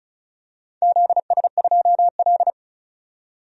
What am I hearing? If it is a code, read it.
Morse code, heard as a keyed tone, about 36 wpm, 690 Hz. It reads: ZS2L
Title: DXCC-100 Call Signs - 4 Characters - 35wpm